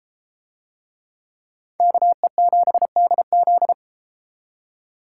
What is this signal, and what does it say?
Morse code, heard as a keyed tone, about 32 wpm, 700 Hz. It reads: KE7DZ